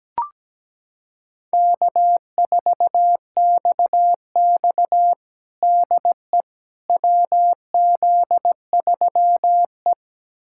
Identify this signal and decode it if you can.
Morse code, heard as a keyed tone, about 17 wpm, 700 Hz. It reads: K4XX DE WZ3E
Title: USA Callsign DE Callsign - 4 Characters - 17wpm